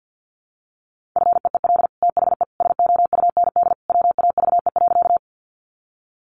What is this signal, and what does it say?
Morse code, heard as a keyed tone, about 50 wpm, 700 Hz. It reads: FEEL THE SOUND WAVE.
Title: Sentences from Top 1400 Words 50wpm